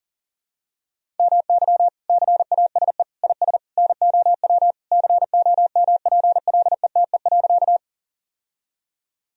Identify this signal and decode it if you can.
Morse code, heard as a keyed tone, about 40 wpm, 700 Hz. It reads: MY CASE IS NOW COMPLETE.
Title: Sentences from Top 1000 Words - Review Part 2 - 40wpm